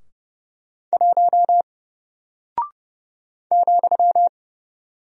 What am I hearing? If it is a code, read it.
Morse code, heard as a keyed tone, about 30 wpm, 700 Hz. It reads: 1 ,
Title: Single Character (Letter:Number:Punctuation:Prosign) Alt 30wpm